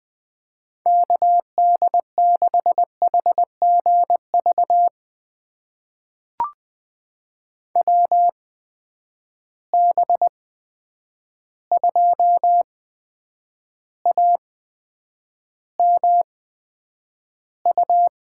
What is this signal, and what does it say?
Morse code, heard as a keyed tone, about 20 wpm, spaced as for 9 wpm, 700 Hz. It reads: KD6HGV WB2AMU
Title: USA Call Signs - 6 Characters - 8x Character Spacing - 20wpm